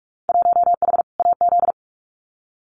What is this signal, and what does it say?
Morse code, heard as a keyed tone, about 44 wpm, 700 Hz. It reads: 1H AZ
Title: Winter Field Day Exchanges 45wpm